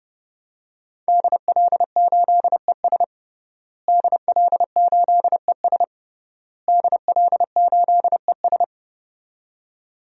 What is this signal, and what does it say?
Morse code, heard as a keyed tone, about 30 wpm, 700 Hz. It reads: DL8ES DL8ES DL8ES
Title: DXCC-100 Call Signs - 5 Characters - Repeated 3x - 30wpm